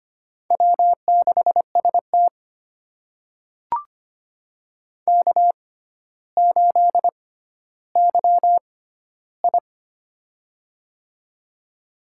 Morse code, heard as a keyed tone, about 25 wpm, spaced as for 13 wpm, 700 Hz. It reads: W6ST K8YI
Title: USA Call Signs - 6x Character Spacing - 25wpm